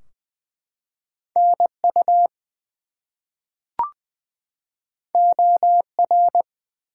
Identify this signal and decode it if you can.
Morse code, heard as a keyed tone, about 20 wpm, 700 Hz. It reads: NU OR
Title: US States and Canadian Provinces 20wpm